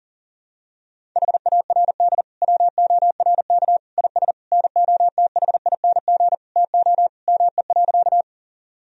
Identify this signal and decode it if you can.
Morse code, heard as a keyed tone, about 40 wpm, 700 Hz. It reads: HARD WORK IS NOTHING TO ME.
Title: Sentences from Top 1000 Words - Review Part 1 - 40wpm